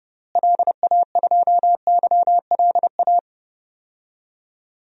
Morse code, heard as a keyed tone, about 31 wpm, 700 Hz. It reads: LA2YLA
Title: DXCC-100 Call Signs - 6 Characters - Repeated 3x - 30wpm